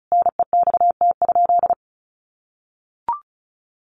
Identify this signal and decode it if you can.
Morse code, heard as a keyed tone, about 34 wpm, 700 Hz. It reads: NEXT?
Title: Sentences from Top 1200 Words 35wpm